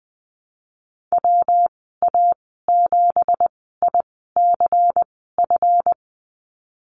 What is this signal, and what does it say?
Morse code, heard as a keyed tone, about 20 wpm, spaced as for 15 wpm, 700 Hz. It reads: WA7ICF
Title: 173 Lesson 28g - Number 9 - Call Signs 20wpm